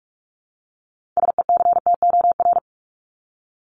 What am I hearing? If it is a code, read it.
Morse code, heard as a keyed tone, about 44 wpm, 700 Hz. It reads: SECTOR
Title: Top 1751-2000 Words 45wpm